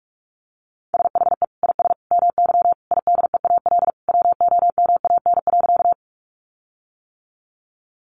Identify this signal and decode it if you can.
Morse code, heard as a keyed tone, about 44 wpm, 700 Hz. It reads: SHE IS MY IDEAL WOMAN.
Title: Sentences from Top 1400 Words 45wpm